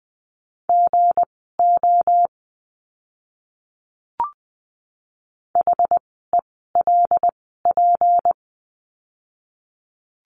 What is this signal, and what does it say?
Morse code, heard as a keyed tone, about 20 wpm, spaced as for 14 wpm, 700 Hz. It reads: GO HELP
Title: 237 Lesson 38f - Letter q - Words 20wpm